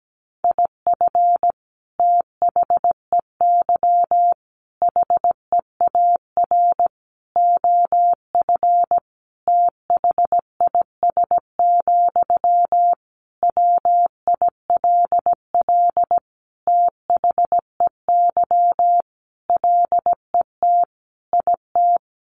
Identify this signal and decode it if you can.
Morse code, heard as a keyed tone, about 17 wpm, 700 Hz. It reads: IF THEY HEAR OF THIS, WILL THEY LET IT
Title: Sentences from Top 400 Words 17wpm